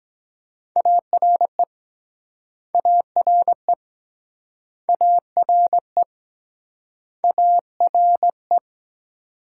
Morse code, heard as a keyed tone, about 23 wpm, 700 Hz. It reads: ARE ARE ARE ARE